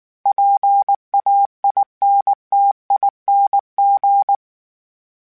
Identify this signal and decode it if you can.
Morse code, heard as a keyed tone, about 19 wpm, 800 Hz. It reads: PAINTING
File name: Special INGs / Special INGs-mp3s / Sngl Ing @18